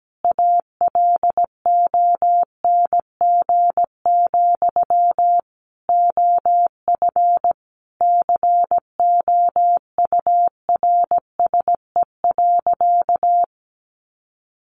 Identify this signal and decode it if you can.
Morse code, heard as a keyed tone, about 17 wpm, 700 Hz. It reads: ALONG, OF COURSE.
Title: Sentences from Top 700 Words 17wpm